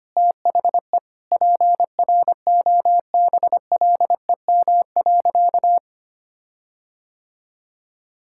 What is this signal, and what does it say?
Morse code, heard as a keyed tone, about 25 wpm, 700 Hz. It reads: THE PROBLEM.
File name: practice / 25wpm / Sentences with 4-Word Phrases - 25wpm